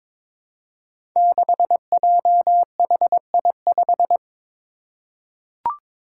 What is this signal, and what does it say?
Morse code, heard as a keyed tone, about 23 wpm, 700 Hz. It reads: 6JHI5
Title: Five Challenging Letter-Number Combinations 22wpm